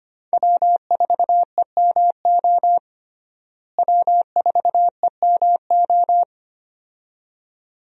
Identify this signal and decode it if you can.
Morse code, heard as a keyed tone, about 25 wpm, 700 Hz. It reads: W4EMO W4EMO
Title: USA Call Signs - 5 Characters - Repeated 2x - 25wpm